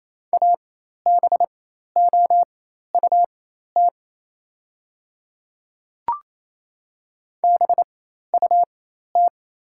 Morse code, heard as a keyed tone, about 27 wpm, spaced as for 13 wpm, 690 Hz. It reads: ABOUT BUT
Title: 4x Character Spacing - Top 200 Words - 28wpm